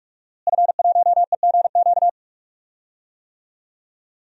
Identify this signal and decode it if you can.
Morse code, heard as a keyed tone, about 43 wpm, 700 Hz. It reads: F1EGQ
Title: DXCC-100 Call Signs - 5 Characters - Repeated 2x - 45wpm